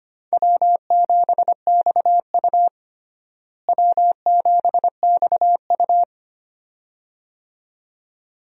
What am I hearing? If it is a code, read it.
Morse code, heard as a keyed tone, about 25 wpm, 700 Hz. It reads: W7XU W7XU
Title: USA Call Signs - 4 Characters - Repeated 2x - 25wpm